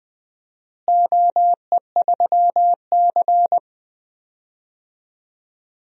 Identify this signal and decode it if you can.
Morse code, heard as a keyed tone, about 21 wpm, 700 Hz. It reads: OE3C
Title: DXCC-100 Call Signs - 4 Characters - Repeated 2x - 20wpm